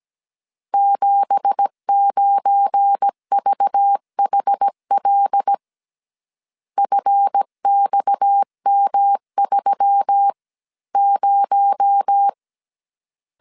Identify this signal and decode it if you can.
Morse code, heard as a keyed tone, about 17 wpm, 800 Hz. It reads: 79VHL FXM30
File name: CW / DET / 1 9 1500C S C 16wpm 2909